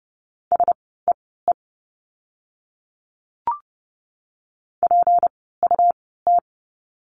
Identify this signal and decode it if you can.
Morse code, heard as a keyed tone, about 30 wpm, spaced as for 17 wpm, 700 Hz. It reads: SEE PUT
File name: course / 30:10wpm / 237 Lesson 38f - Letter q - Words 30wpm